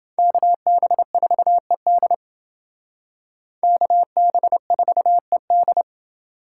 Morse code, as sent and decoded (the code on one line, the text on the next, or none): -.- -... ....- . -.. / -.- -... ....- . -..
KB4ED KB4ED